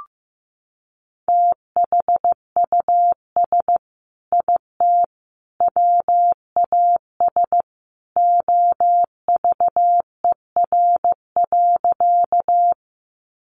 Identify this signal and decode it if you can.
Morse code, heard as a keyed tone, about 15 wpm, 700 Hz. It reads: THUS IT WAS OVER.
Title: Sentences from Top 800 Words 15wpm